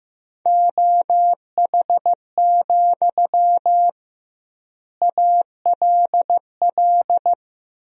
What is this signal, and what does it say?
Morse code, heard as a keyed tone, about 15 wpm, 700 Hz. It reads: OH, ALL
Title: 2x Word-Spacing - Sentences from Top 500 Words - 15wpm